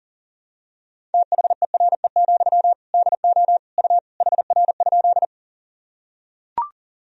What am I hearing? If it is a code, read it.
Morse code, heard as a keyed tone, about 40 wpm, 700 Hz. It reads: THERE, DO U HR?
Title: Sentences from Top 300 Words and Abbreviations 40wpm